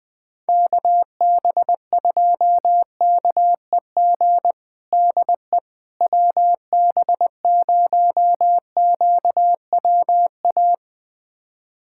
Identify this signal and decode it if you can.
Morse code, heard as a keyed tone, about 21 wpm, 690 Hz. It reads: KB2KEG DE WB0QWA
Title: USA Callsign DE Callsign 20wpm